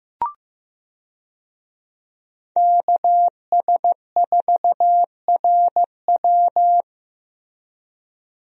Morse code, heard as a keyed tone, about 15 wpm, 700 Hz. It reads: KS4RW
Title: USA Call Signs - 5 Characters - Repeated 3x - 15wpm